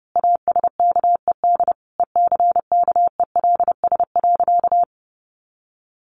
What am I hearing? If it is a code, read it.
Morse code, heard as a keyed tone, about 30 wpm, 700 Hz. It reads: ASKED ECKELS.
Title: A Sound Of Thunder 30wpm